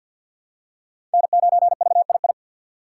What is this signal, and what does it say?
Morse code, heard as a keyed tone, about 53 wpm, 700 Hz. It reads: N9VII
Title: USA Call Signs - 4x Character Spacing - 50wpm